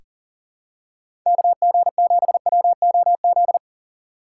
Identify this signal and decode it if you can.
Morse code, heard as a keyed tone, about 38 wpm, 700 Hz. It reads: KG7WOZ
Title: USA Call Signs - 6 Characters - 6x Character Spacing - 40wpm